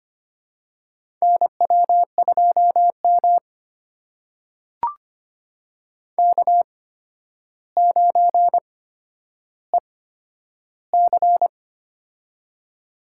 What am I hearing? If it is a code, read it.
Morse code, heard as a keyed tone, about 25 wpm, spaced as for 11 wpm, 700 Hz. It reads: NW2M K9EC
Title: USA Call Signs - 4 Characters - 8x Character Spacing - 25wpm